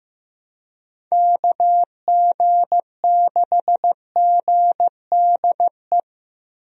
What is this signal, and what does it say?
Morse code, heard as a keyed tone, about 16 wpm, 700 Hz. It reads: KG6GDE
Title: USA Call Signs - 6 Characters - Repeated 2x - 15wpm